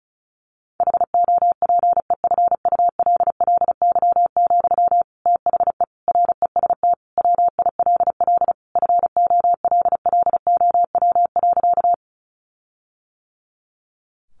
Morse code, heard as a keyed tone, about 34 wpm, 700 Hz. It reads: HOPEFULLY, THE REST WILL FOLLOW.
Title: Sentences from Top 1500 Words 35wpm